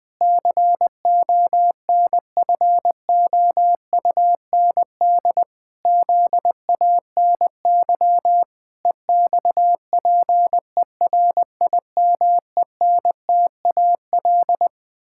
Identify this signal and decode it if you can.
Morse code, heard as a keyed tone, about 20 wpm, 700 Hz. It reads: CONFOUND ZANY EXPERIMENTAL
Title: Pangram Challenge v2 - 20wpm